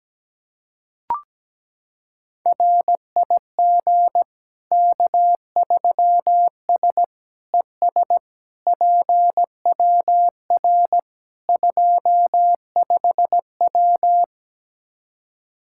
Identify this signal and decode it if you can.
Morse code, heard as a keyed tone, about 17 wpm, 700 Hz. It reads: RIG K3S ES PWR 25W
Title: Practice QSO - Exchange 2 - Info - 17wpm